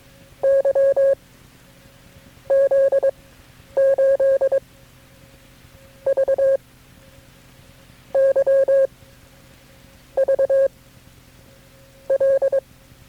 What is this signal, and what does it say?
Morse code, heard as a keyed tone, about 22 wpm, 540 Hz. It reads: Y Z 8 V Y V L
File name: FreeFiles / CW-practice / KB6MT-tape3-side1